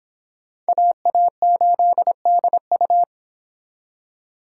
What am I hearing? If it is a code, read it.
Morse code, heard as a keyed tone, about 30 wpm, 700 Hz. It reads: AA8DU